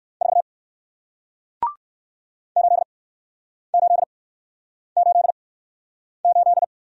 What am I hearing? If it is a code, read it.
Morse code, heard as a keyed tone, about 60 wpm, spaced as for 9 wpm, 700 Hz. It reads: 4 8888